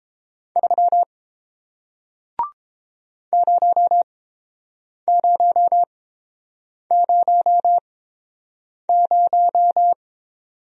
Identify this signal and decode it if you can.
Morse code, heard as a keyed tone, about 30 wpm, spaced as for 7 wpm, 700 Hz. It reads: 3 0000